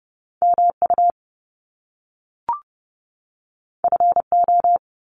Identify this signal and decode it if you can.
Morse code, heard as a keyed tone, about 30 wpm, 700 Hz. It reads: MU FO
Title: Top Two Letter Combinations in Words 30wpm